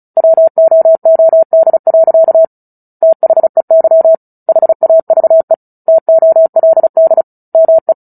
Morse code, heard as a keyed tone, about 34 wpm, 650 Hz. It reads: WOOD. THEY HAVE TOLD ME
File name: Book Two / 09-Chapter 9 - 35 WPM